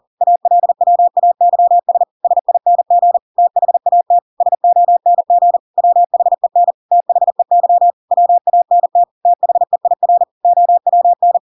Morse code, heard as a keyed tone, about 40 wpm, 700 Hz. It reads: ALWAYS SING THAT SONG WHEN THEY WANT THEIR OWN